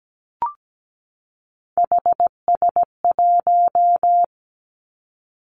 Morse code, heard as a keyed tone, about 17 wpm, 700 Hz. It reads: HS1
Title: Three Challenging Letter-Number Combinations 17wpm